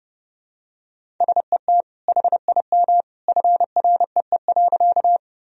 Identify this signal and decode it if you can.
Morse code, heard as a keyed tone, about 29 wpm, 700 Hz. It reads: SET HIM FREE.
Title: Sentences from Top 1000 Words - Review Part 1 - 30wpm